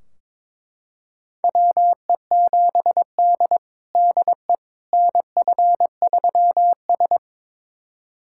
Morse code, heard as a keyed tone, about 21 wpm, 700 Hz. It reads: WE7D DE NF3S
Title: USA Callsign DE Callsign - 4 Characters - 22wpm